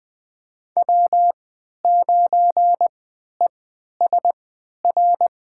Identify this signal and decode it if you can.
Morse code, heard as a keyed tone, about 20 wpm, spaced as for 11 wpm, 700 Hz. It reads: W9ESR
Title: USA Call Signs - 5 Characters - 3x Character Spacing - 20wpm